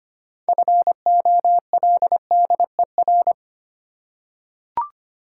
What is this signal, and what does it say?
Morse code, heard as a keyed tone, about 25 wpm, 700 Hz. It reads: FOLDER
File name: practice / 25wpm / Difficult Words with Hints - 25wpm